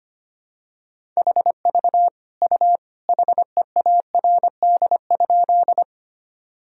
Morse code, heard as a keyed tone, about 25 wpm, 700 Hz. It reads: HV U HEARD?
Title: Sentences from Top 400 Words and Abbreviations 25wpm